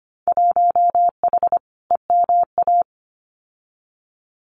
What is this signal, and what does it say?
Morse code, heard as a keyed tone, about 25 wpm, 690 Hz. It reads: 1H EMA
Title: Winter Field Day Exchanges 25wpm